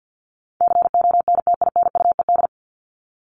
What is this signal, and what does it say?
Morse code, heard as a keyed tone, about 50 wpm, 700 Hz. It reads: CONTINUED
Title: ED Suffix - Top 100 Words with Base Words 50wpm